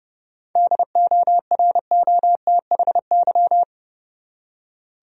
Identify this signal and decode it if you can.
Morse code, heard as a keyed tone, about 31 wpm, 700 Hz. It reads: DOROTHY
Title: USA Common Names - 30wpm